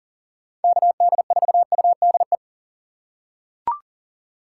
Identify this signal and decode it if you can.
Morse code, heard as a keyed tone, about 40 wpm, 700 Hz. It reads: KD4UDE
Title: USA Call Signs - 6 Characters - 3x Character Spacing - 40wpm